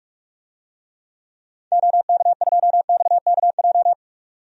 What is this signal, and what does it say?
Morse code, heard as a keyed tone, about 46 wpm, 700 Hz. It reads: OK2XKJ
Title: DXCC-100 Call Signs - 6 Characters - 45wpm